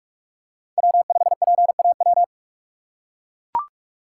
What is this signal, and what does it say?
Morse code, heard as a keyed tone, about 44 wpm, 700 Hz. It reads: W5PAW
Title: USA Call Signs - 5 Characters - 3x Character Spacing - 45wpm